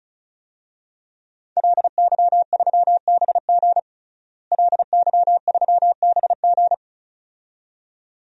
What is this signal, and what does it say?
Morse code, heard as a keyed tone, about 36 wpm, 700 Hz. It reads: LY3BG LY3BG
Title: DXCC-100 Call Signs - 5 Characters - Repeated 2x - 35wpm